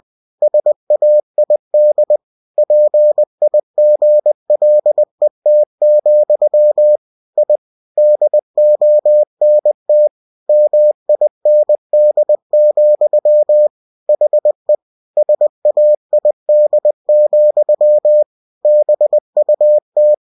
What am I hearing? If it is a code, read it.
Morse code, heard as a keyed tone, about 20 wpm, 600 Hz. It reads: SAID PIGLET, I DONT MIND, HE SAID, BUT
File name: cw-books / winnie-the-pooh / winnie-20wpm-page-0065